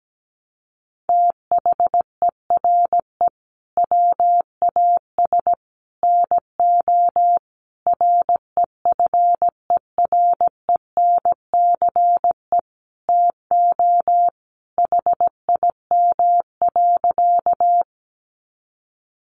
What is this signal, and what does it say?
Morse code, heard as a keyed tone, about 17 wpm, 700 Hz. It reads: THERE WAS NO REFERENCE TO HIM.
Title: Sentences from Top 1200 Words 17wpm